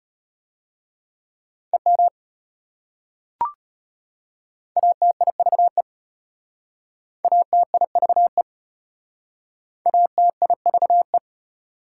Morse code, heard as a keyed tone, about 34 wpm, 700 Hz. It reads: EM ATIVE ATIVE ATIVE